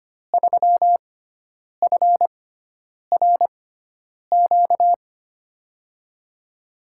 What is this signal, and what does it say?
Morse code, heard as a keyed tone, about 25 wpm, spaced as for 9 wpm, 700 Hz. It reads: 3FRQ
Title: USA Call Signs - 6x Character Spacing - 25wpm